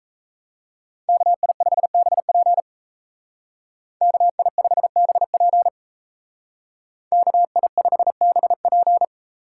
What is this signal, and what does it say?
Morse code, heard as a keyed tone, about 34 wpm, 700 Hz. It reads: KI5BP KI5BP KI5BP
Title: Speed Racing - USA 2x2 Call Signs 28wpm